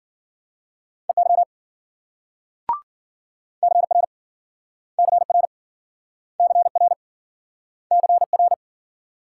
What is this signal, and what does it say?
Morse code, heard as a keyed tone, about 54 wpm, 700 Hz. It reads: EY CR CR CR CR